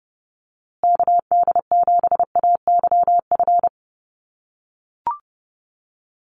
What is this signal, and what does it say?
Morse code, heard as a keyed tone, about 29 wpm, 700 Hz. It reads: KD7AYF
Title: USA Call Signs Repeated 3x 30wpm